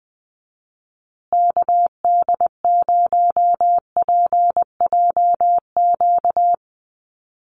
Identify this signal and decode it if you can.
Morse code, heard as a keyed tone, about 20 wpm, 700 Hz. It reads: KD0PJQ
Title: USA Call Signs Repeated 2x 20wpm